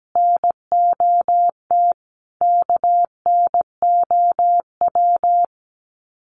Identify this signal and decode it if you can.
Morse code, heard as a keyed tone, about 17 wpm, 700 Hz. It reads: NOT KNOW
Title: Sets of 5 Words Encore 17wpm